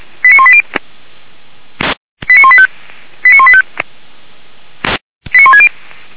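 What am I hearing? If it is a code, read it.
Five-tone selective call (ZVEI), heard as quick runs of five tones, tones of about 70 ms. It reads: 89180 89185 89185 89160
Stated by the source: Five-tone ZVEI selective call received on an SDR.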